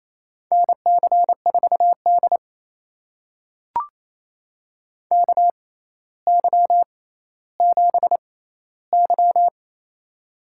Morse code, heard as a keyed tone, about 29 wpm, spaced as for 15 wpm, 700 Hz. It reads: NC4D KY7Y
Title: USA Call Signs - 4 Characters - 6x Character Spacing - 28wpm